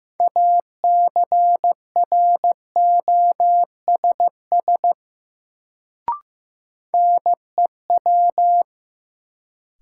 Morse code, heard as a keyed tone, about 15 wpm, 700 Hz. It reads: ACROSS NEW
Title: Sets of 2 Words - New Word Drill - 15wpm